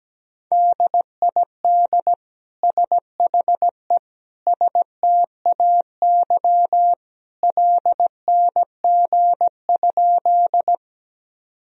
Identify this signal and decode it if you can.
Morse code, heard as a keyed tone, about 17 wpm, 700 Hz. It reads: DID SHE STAY LNG?
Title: Sentences from Top 500 Words and Abbreviations 17wpm